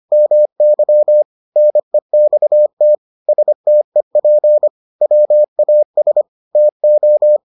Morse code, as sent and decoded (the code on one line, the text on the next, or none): -- -.-- / -. -..- - / ... - . .--. / .-- .- ... / - ---
MY NEXT STEP WAS TO